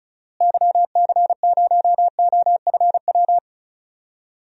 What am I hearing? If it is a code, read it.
Morse code, heard as a keyed tone, about 34 wpm, 700 Hz. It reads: YC0OFW
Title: DXCC-100 Call Signs - 6 Characters - Repeated 3x - 35wpm